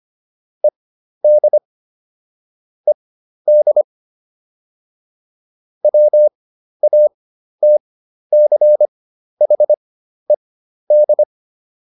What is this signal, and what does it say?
Morse code, heard as a keyed tone, about 25 wpm, spaced as for 12 wpm, 600 Hz. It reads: ED ED WATCHED